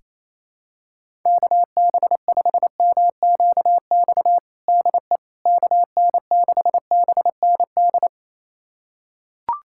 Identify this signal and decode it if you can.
Morse code, heard as a keyed tone, about 28 wpm, 700 Hz. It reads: KB5MQX DE KN6BND
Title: USA Callsign DE Callsign 28wpm